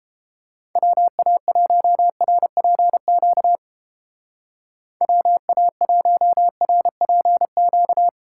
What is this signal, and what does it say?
Morse code, heard as a keyed tone, about 31 wpm, 700 Hz. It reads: WA1RPQ WA1RPQ